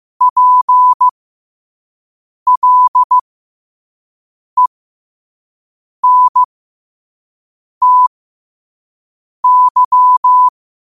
Morse code, heard as a keyed tone, about 15 wpm, spaced as for 5 wpm, 1000 Hz. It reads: PLENTY
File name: qotdfiles / 5wpm